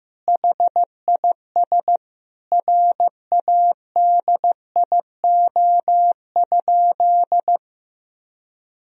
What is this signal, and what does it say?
Morse code, heard as a keyed tone, about 15 wpm, 700 Hz. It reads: HIS RADIO?
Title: Sentences from Top 1000 Words - Review Part 3 - 15wpm